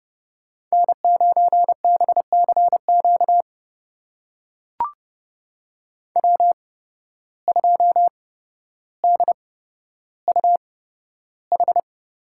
Morse code, heard as a keyed tone, about 30 wpm, spaced as for 13 wpm, 700 Hz. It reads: N9BCQ W2DUH
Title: USA Call Signs - 5 Characters - 8x Character Spacing - 30wpm